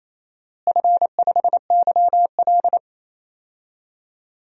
Morse code, heard as a keyed tone, about 27 wpm, 700 Hz. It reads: F5YL
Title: DXCC-100 Call Signs - 4 Characters - Repeated 2x - 28wpm